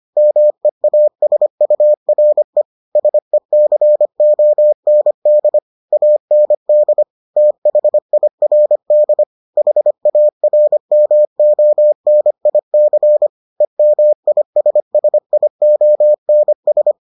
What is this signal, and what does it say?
Morse code, heard as a keyed tone, about 25 wpm, 600 Hz. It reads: MEASURE SECOND AND THIRD HARMONIC EMISSIONS